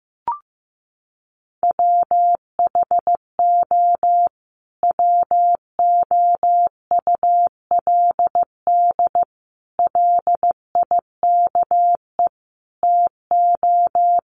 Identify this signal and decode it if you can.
Morse code, heard as a keyed tone, about 15 wpm, 700 Hz. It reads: WHO WOULD LIKE TO
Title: Sentences from Top 1500 Words 15wpm